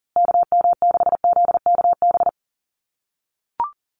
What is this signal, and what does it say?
Morse code, heard as a keyed tone, about 38 wpm, 700 Hz. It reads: KM6ZKB
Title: USA Call Signs Repeated 2x 40wpm